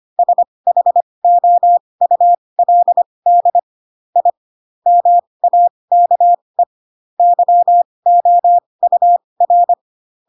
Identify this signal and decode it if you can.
Morse code, heard as a keyed tone, about 25 wpm, spaced as for 20 wpm, 700 Hz. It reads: SHOULD I MAKE YOUR
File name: Pinocchio / Pinocchio ch8 15 25 ik1hge